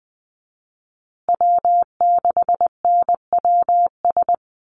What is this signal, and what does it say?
Morse code, heard as a keyed tone, about 19 wpm, 700 Hz. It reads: W6NWS
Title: Call Signs Repeated 3x - 20wpm